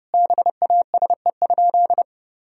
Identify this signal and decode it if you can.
Morse code, heard as a keyed tone, about 30 wpm, 700 Hz. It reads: BASE?
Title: Sentences from Top 1000 Words - Review Part 1 - 30wpm